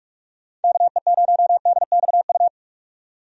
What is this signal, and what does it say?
Morse code, heard as a keyed tone, about 46 wpm, 700 Hz. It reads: KE0DXU
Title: USA Call Signs - 6 Characters - 45wpm